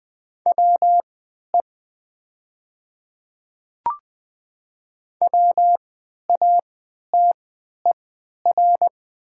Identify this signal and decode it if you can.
Morse code, heard as a keyed tone, about 20 wpm, spaced as for 11 wpm, 700 Hz. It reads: WE WATER
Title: 3x Character Spacing - Top 100 Words - 20wpm